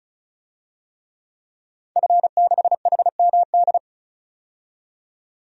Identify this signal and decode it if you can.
Morse code, heard as a keyed tone, about 36 wpm, 700 Hz. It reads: F6HMD
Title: DXCC-100 Call Signs - 5 Characters - 35wpm